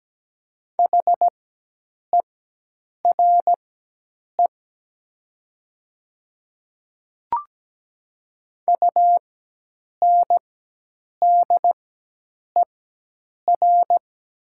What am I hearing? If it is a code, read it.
Morse code, heard as a keyed tone, about 17 wpm, spaced as for 8 wpm, 700 Hz. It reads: HERE UNDER
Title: 4x Character Spacing - Top 200 Words - 17wpm